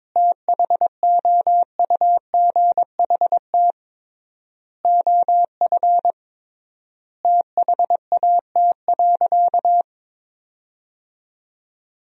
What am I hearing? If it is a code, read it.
Morse code, heard as a keyed tone, about 22 wpm, 700 Hz. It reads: THOUGHT OF THAT.
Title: 3x Word-Spacing - Sentences from Top 300 Words - 22wpm